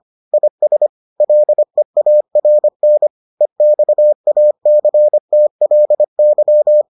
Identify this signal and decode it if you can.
Morse code, heard as a keyed tone, about 25 wpm, 600 Hz. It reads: IS LEARN EXACTLY